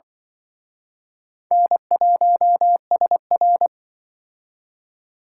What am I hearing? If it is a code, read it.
Morse code, heard as a keyed tone, about 25 wpm, 700 Hz. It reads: N1SR